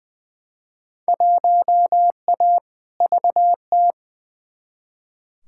Morse code, heard as a keyed tone, about 20 wpm, 700 Hz. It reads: 1A VT
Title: Field Day Exchanges Encore - 20wpm